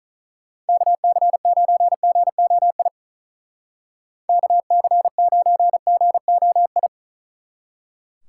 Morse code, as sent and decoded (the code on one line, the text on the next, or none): -.- -.-. ----. --. --- .. / -.- -.-. ----. --. --- ..
KC9GOI KC9GOI